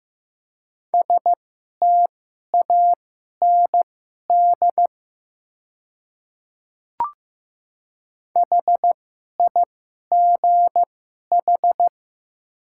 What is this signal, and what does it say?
Morse code, heard as a keyed tone, about 15 wpm, spaced as for 11 wpm, 700 Hz. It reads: STAND HIGH
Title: 2x Character Spacing - Top 300 Words - 15wpm